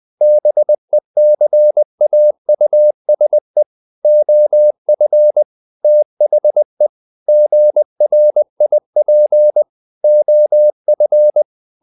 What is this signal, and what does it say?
Morse code, heard as a keyed tone, about 20 wpm, 600 Hz. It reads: BECAUSE OF THE GRIP OF
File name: the old man and the sea part 2 20wpm